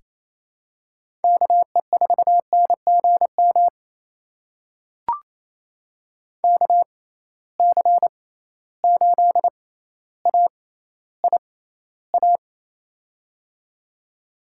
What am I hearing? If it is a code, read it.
Morse code, heard as a keyed tone, about 28 wpm, spaced as for 14 wpm, 700 Hz. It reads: KE4NGM KC8AIA
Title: USA Call Signs - 6 Characters - 6x Character Spacing - 28wpm